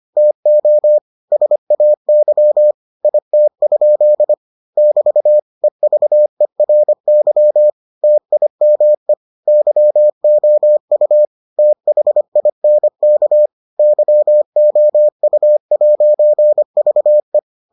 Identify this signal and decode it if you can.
Morse code, heard as a keyed tone, about 25 wpm, 600 Hz. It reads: TO SAY IT? <BT> EVERY TIME YOU THINK YOU'VE